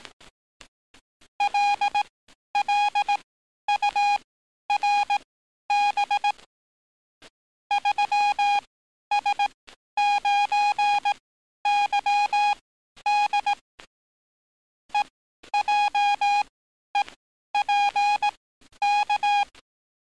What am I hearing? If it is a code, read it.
Morse code, heard as a keyed tone, about 18 wpm, spaced as for 11 wpm, 820 Hz. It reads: LLURB 3S9YD EJEPK
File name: training / NAVPERS 11049to11060 / NAVPERS 11050 RA RB